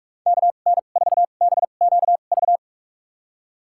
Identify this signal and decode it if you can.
Morse code, heard as a keyed tone, about 46 wpm, spaced as for 32 wpm, 700 Hz. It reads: KN4BQV
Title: USA Call Signs - 6 Characters - 2x Character Spacing - 45wpm